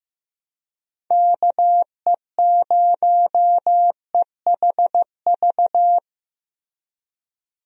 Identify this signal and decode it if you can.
Morse code, heard as a keyed tone, about 15 wpm, 700 Hz. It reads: KE0EHV
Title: USA Call Signs - 6 Characters - Repeated 2x - 15wpm